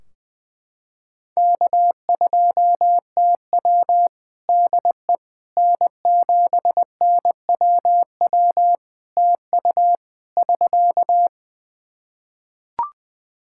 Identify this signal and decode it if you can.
Morse code, heard as a keyed tone, about 21 wpm, 700 Hz. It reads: K2TW DE N7NWW TU <SK>
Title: K1USN SST QSO Practice 20wpm